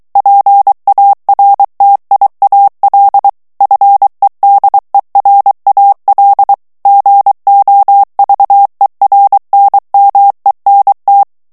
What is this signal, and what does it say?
Morse code, heard as a keyed tone, about 23 wpm, 800 Hz. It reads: PARTIAL FEDERAL GOVERNMENT
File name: ANGLAIS texte clair 3 vitesse 22 mots minute